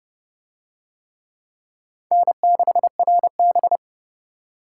Morse code, heard as a keyed tone, about 29 wpm, 700 Hz. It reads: N6RB